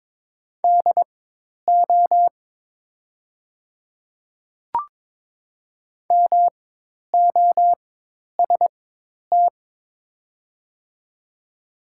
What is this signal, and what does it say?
Morse code, heard as a keyed tone, about 22 wpm, spaced as for 10 wpm, 700 Hz. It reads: DO MOST
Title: 4x Character Spacing - Top 100 Words - 22wpm